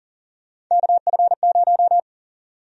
Morse code, heard as a keyed tone, about 39 wpm, 700 Hz. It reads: KF0
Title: USA Call Sign Prefixes - Repeated 2x - 40wpm